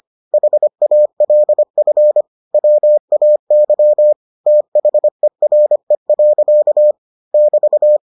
Morse code, heard as a keyed tone, about 25 wpm, 600 Hz. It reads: HALF WAY THERE. <BT>